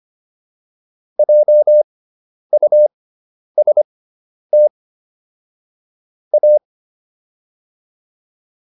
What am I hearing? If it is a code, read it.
Morse code, heard as a keyed tone, about 25 wpm, spaced as for 10 wpm, 600 Hz. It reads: JUST A